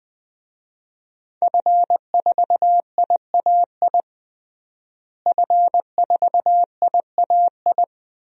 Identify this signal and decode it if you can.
Morse code, heard as a keyed tone, about 20 wpm, 700 Hz. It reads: F4IAI F4IAI
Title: DXCC-100 Call Signs - 5 Characters - Repeated 3x - 20wpm